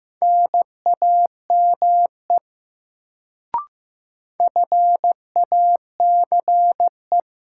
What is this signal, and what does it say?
Morse code, heard as a keyed tone, about 16 wpm, 700 Hz. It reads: NAME FACE
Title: Sets of 3 Words - New Word Drill - 15wpm